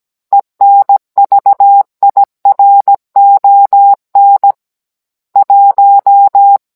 Morse code, heard as a keyed tone, about 17 wpm, 800 Hz. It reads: ENVIRON 1
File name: Cours-NR18-16mm